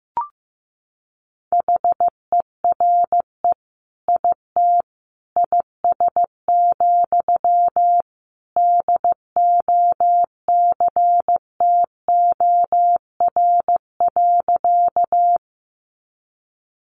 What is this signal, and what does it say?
Morse code, heard as a keyed tone, about 15 wpm, 700 Hz. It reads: HERE IT IS, DOCTOR.
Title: Sentences from Top 1500 Words 15wpm